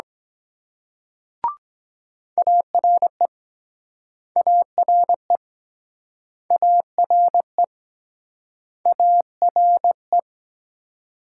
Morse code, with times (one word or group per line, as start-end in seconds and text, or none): 2.4-3.3 ARE
4.4-5.4 ARE
6.5-7.6 ARE
8.9-10.2 ARE